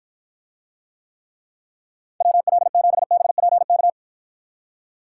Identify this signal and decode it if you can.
Morse code, heard as a keyed tone, about 50 wpm, 700 Hz. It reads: WL7BPK